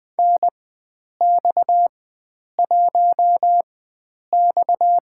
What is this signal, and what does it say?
Morse code, heard as a keyed tone, about 20 wpm, spaced as for 9 wpm, 700 Hz. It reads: NX1X
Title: USA Call Signs - 4x Character Spacing - 20wpm